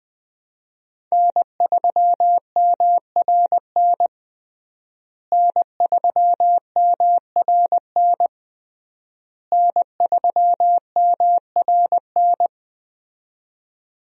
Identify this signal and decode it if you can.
Morse code, heard as a keyed tone, about 21 wpm, 700 Hz. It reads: N3MRN N3MRN N3MRN
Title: USA Call Signs - 5 Characters - Repeated 3x - 20wpm